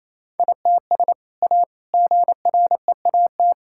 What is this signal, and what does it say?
Morse code, heard as a keyed tone, about 28 wpm, 700 Hz. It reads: ITS A GREAT
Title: Sentences from Top 1000 Words - Review Part 6 - 28wpm